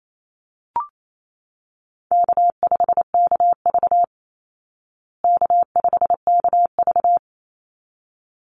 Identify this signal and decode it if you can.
Morse code, heard as a keyed tone, about 28 wpm, 700 Hz. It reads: K5KV K5KV
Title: Call Signs Repeated 3x - 28wpm